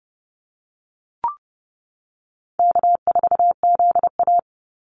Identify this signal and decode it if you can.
Morse code, heard as a keyed tone, about 30 wpm, 700 Hz. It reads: K4ZA
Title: Easy Call Signs Repeated 2x - 30wpm